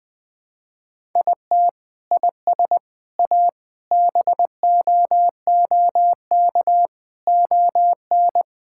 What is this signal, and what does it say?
Morse code, heard as a keyed tone, about 21 wpm, 700 Hz. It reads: IT IS A BOOK ON
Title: Sentences from Top 1000 Words - Review Part 5 - 20wpm